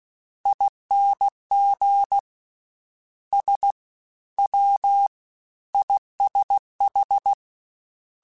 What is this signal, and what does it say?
Morse code, heard as a keyed tone, about 16 wpm, 800 Hz. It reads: ING S W ISH
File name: Misc / Misc-mp3s / Jumps @15